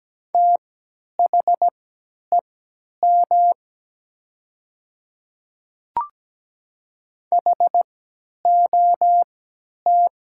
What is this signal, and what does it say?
Morse code, heard as a keyed tone, about 17 wpm, spaced as for 10 wpm, 700 Hz. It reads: THEM HOT
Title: 3x Character Spacing - Top 300 Words - 17wpm